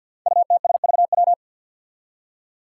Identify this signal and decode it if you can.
Morse code, heard as a keyed tone, about 47 wpm, 700 Hz. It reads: UTSVW
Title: Five Challenging Letter-Number Combinations 50wpm